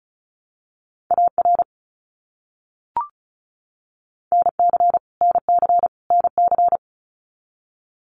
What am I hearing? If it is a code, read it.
Morse code, heard as a keyed tone, about 34 wpm, 700 Hz. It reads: AR NC NC NC
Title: US States Repeated 3x 35wpm